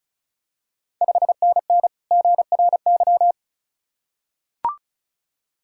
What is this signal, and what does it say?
Morse code, heard as a keyed tone, about 36 wpm, 690 Hz. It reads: 5NN GRY
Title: Ontario QSO Party Exchange 35wpm